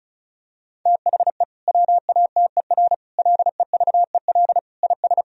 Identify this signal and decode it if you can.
Morse code, heard as a keyed tone, about 36 wpm, 700 Hz. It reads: THE WATER LEVEL IS
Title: Sentences from Top 1000 Words - Review Part 4 - 35wpm